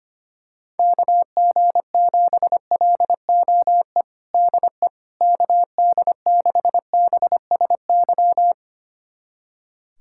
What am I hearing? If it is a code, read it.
Morse code, heard as a keyed tone, about 25 wpm, 700 Hz. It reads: KG7LOE DE KD6BSY
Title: USA Callsign DE Callsign - 6 Characters - 25wpm